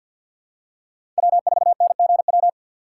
Morse code, heard as a keyed tone, about 53 wpm, 700 Hz. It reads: W4NGW
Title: USA Call Signs - 5 Characters - 4x Character Spacing - 50wpm